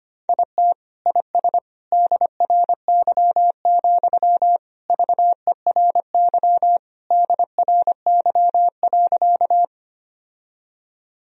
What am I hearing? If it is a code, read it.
Morse code, heard as a keyed tone, about 25 wpm, 700 Hz. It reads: IT IS DRY, VERY DRY.
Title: Sentences from Top 1000 Words - Review Part 3 - 25wpm